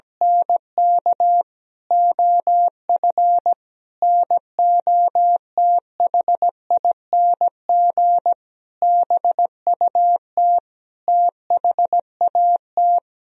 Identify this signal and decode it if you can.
Morse code, heard as a keyed tone, about 17 wpm, 700 Hz. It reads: NK OF NOTHING BUT THAT
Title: Sentences from Top 1000 Words - Review Part 1 - 17wpm